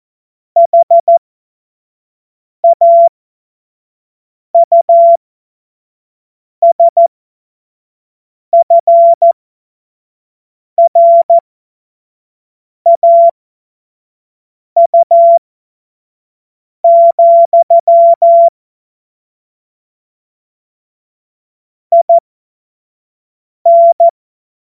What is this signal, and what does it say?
Morse code, heard as a keyed tone, about 14 wpm, spaced as for 5 wpm, 680 Hz. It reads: HAUSFRAU, IN